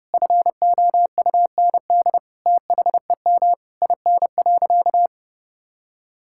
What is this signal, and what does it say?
Morse code, heard as a keyed tone, about 28 wpm, 700 Hz. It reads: FOUND THEM IN.
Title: Sentences from Top 1000 Words - Review Part 6 - 30wpm